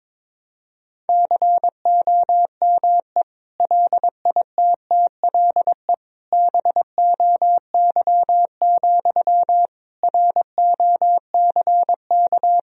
Morse code, heard as a keyed tone, about 22 wpm, 700 Hz. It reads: COME LITTLE BOY, ROCK
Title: Sentences from Top 1000 Words - Review Part 2 - 22wpm